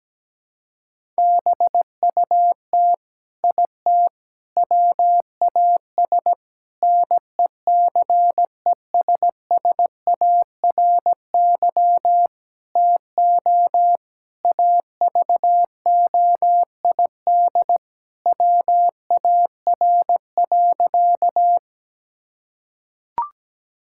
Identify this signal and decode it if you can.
Morse code, heard as a keyed tone, about 17 wpm, 700 Hz. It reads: BUT IT WAS NECESSARY TO AVOID WAR.
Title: Sentences from Top 1000 Words - Review Part 6 - 17wpm